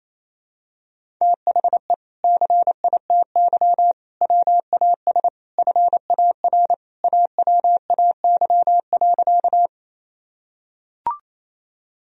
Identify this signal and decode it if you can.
Morse code, heard as a keyed tone, about 28 wpm, 700 Hz. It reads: THE CITY WAS FAR AWAY.
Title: Sentences from Top 1000 Words - Review Part 1 - 28wpm